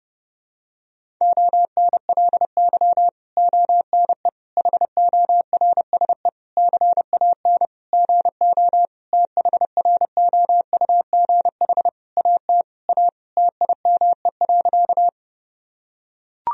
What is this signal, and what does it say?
Morse code, heard as a keyed tone, about 30 wpm, 700 Hz. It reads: ONLY ONE HORSE CAN GO THROUGH AT A TIME.
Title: Sentences from Top 1000 Words - Review Part 1 - 30wpm